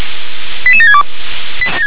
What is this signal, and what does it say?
Five-tone selective call (ZVEI), heard as quick runs of five tones, tones of about 70 ms. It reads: 88762
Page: ZVEI